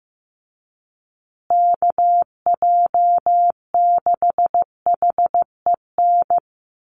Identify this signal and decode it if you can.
Morse code, heard as a keyed tone, about 15 wpm, 700 Hz. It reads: KJ6HEN
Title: USA Call Signs Repeated 2x 15wpm